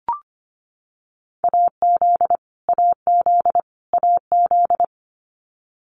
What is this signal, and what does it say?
Morse code, heard as a keyed tone, about 25 wpm, 700 Hz. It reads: AZ AZ AZ
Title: US States Repeated 3x 25wpm